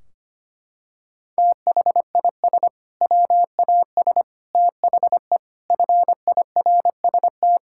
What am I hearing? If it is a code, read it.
Morse code, heard as a keyed tone, about 25 wpm, 700 Hz. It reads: THIS WAS THE FIRST
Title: Sentences from Top 1600 Words 25wpm